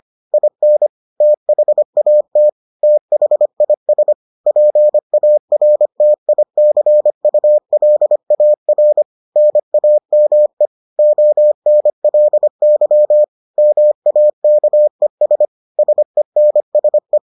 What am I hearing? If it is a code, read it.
Morse code, heard as a keyed tone, about 25 wpm, 600 Hz. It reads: IN THAT THIS PARTICULAR NAME ONLY MAKES SENSE